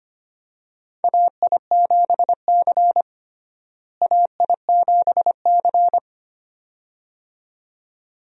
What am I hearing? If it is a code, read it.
Morse code, heard as a keyed tone, about 25 wpm, 700 Hz. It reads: AI7C AI7C
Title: USA Call Signs - 4 Characters - Repeated 2x - 25wpm